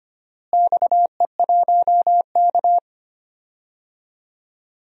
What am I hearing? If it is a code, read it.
Morse code, heard as a keyed tone, about 25 wpm, 700 Hz. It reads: XE1K
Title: DXCC-100 Call Signs - 4 Characters - Repeated 2x - 25wpm